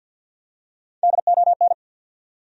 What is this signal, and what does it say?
Morse code, heard as a keyed tone, about 48 wpm, 700 Hz. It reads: DON
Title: USA Common Names - Max 3 Letters - 50wpm